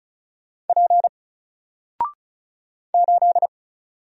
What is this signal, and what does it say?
Morse code, heard as a keyed tone, about 36 wpm, 700 Hz. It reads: P 8
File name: Single-Letter-Number-35wpm